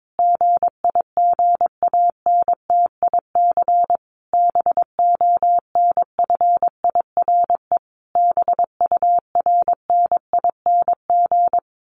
Morse code, heard as a keyed tone, about 22 wpm, 700 Hz. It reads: GIGANTIC BONFIRE BURNING
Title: A Sound Of Thunder 22wpm